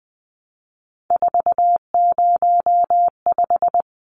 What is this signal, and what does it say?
Morse code, heard as a keyed tone, about 20 wpm, 700 Hz. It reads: 405
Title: Numbers - 3-Digit 20wpm